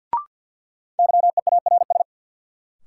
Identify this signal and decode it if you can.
Morse code, heard as a keyed tone, about 51 wpm, 700 Hz. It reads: YEARS
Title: Sentences from Top 200 Words - New Word Drill - 50wpm